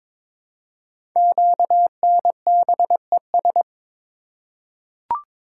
Morse code, heard as a keyed tone, about 22 wpm, 700 Hz. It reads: QNBES
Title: Code Groups - 5 Letters - 22wpm